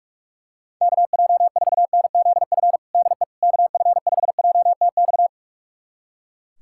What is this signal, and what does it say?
Morse code, heard as a keyed tone, about 41 wpm, 700 Hz. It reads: KJ4NZF DE KF5JTX
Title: USA Callsign DE Callsign - 6 Characters - 45wpm